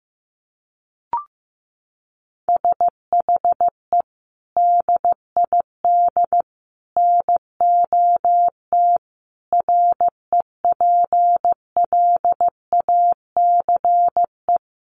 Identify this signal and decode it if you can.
Morse code, heard as a keyed tone, about 15 wpm, 700 Hz. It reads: SHE DID NOT REPLACE
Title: Sentences from Top 1400 Words 15wpm